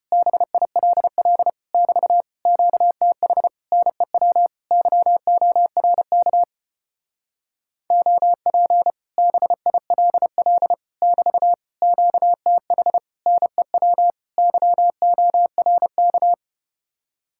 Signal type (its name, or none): Morse code